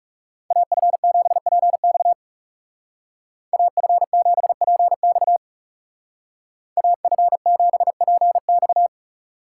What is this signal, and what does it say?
Morse code, heard as a keyed tone, about 35 wpm, 700 Hz. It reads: AF7PX AF7PX AF7PX
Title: Speed Racing - USA 2x2 Call Signs 30wpm